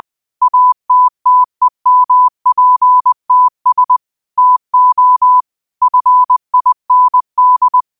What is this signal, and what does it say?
Morse code, heard as a keyed tone, about 20 wpm, 1000 Hz. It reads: ATTEMPTS TO FIND
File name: qotdfiles / 20wpm